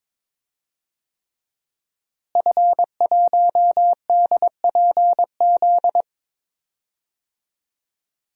Morse code, heard as a keyed tone, about 21 wpm, 700 Hz. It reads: F1DPZ
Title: DXCC-100 Call Signs - 5 Characters - Repeated 3x - 22wpm